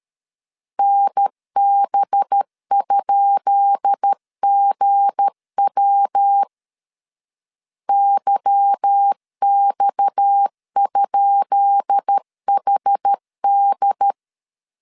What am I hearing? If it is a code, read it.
Morse code, heard as a keyed tone, about 13 wpm, 800 Hz. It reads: NB?GW YX?HD
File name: LEZIONI / 10 SIGNS / 10B INT 12wpm 2414